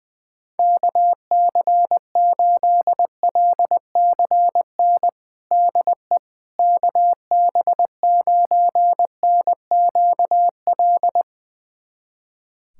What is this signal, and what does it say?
Morse code, heard as a keyed tone, about 20 wpm, 700 Hz. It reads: KC8LCN DE KB9NQL
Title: USA Callsign DE Callsign - 6 Characters - 20wpm